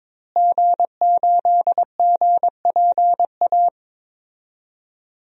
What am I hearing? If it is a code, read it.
Morse code, heard as a keyed tone, about 22 wpm, 700 Hz. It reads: G8GPA
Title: DXCC-100 Call Signs - 5 Characters - Repeated 2x - 22wpm